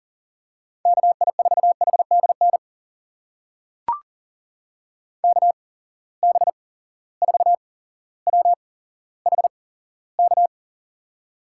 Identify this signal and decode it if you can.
Morse code, heard as a keyed tone, about 39 wpm, spaced as for 17 wpm, 700 Hz. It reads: KI4HDN KB4WHK
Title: USA Call Signs - 6 Characters - 8x Character Spacing - 40wpm